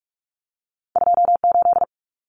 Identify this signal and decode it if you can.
Morse code, heard as a keyed tone, about 44 wpm, 700 Hz. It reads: 28
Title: Numbers - 2-Digit 45wpm